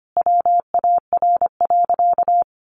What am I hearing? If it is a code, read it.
Morse code, heard as a keyed tone, about 25 wpm, 700 Hz. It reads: WAR.
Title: Sentences from Top 1500 Words 25wpm